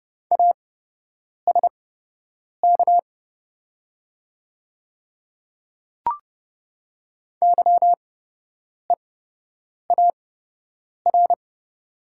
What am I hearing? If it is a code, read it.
Morse code, heard as a keyed tone, about 30 wpm, spaced as for 8 wpm, 700 Hz. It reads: ASK YEAR
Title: 8x Character Spacing - Top 300 Words - 30wpm